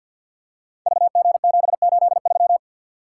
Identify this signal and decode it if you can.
Morse code, heard as a keyed tone, about 52 wpm, 700 Hz. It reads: VG783
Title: Code Groups - 5 Letter:Numbers - 50wpm